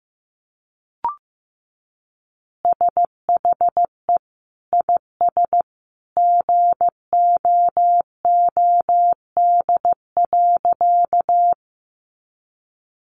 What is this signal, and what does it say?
Morse code, heard as a keyed tone, about 14 wpm, 700 Hz. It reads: SHE IS GOOD.
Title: Easy Phrases 15wpm